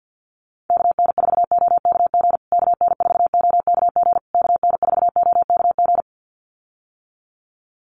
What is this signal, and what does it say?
Morse code, heard as a keyed tone, about 50 wpm, 700 Hz. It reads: KN4OKG KN4OKG KN4OKG
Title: USA Call Signs Repeated 3x 50wpm